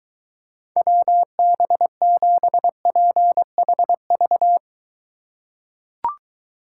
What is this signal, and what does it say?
Morse code, heard as a keyed tone, about 23 wpm, 700 Hz. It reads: WB7PHV